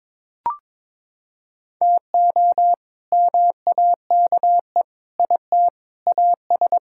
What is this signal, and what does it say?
Morse code, heard as a keyed tone, about 22 wpm, 700 Hz. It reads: TO MAKE IT AS
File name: practice / 22wpm / Sets of 7 Words - 22wpm